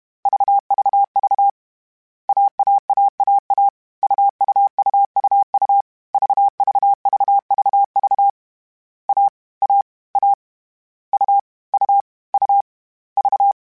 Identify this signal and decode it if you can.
Morse code, heard as a keyed tone, about 32 wpm, 800 Hz. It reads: VVV AAAAA UUUUU VVVVV A A A U U U V
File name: Difficult Combos / Difficult Combos-mp3s / AUV AUV4 @30